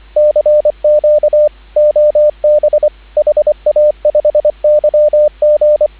Morse code, heard as a keyed tone, about 25 wpm, 590 Hz, with radio noise. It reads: CQ OB HA5YG
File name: CW / 2017 HA-OB HA5YG